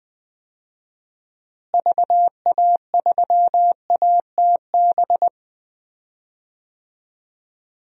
Morse code, heard as a keyed tone, about 20 wpm, 700 Hz. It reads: VA3ATB
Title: DXCC-100 Call Signs - 6 Characters - Repeated 3x - 20wpm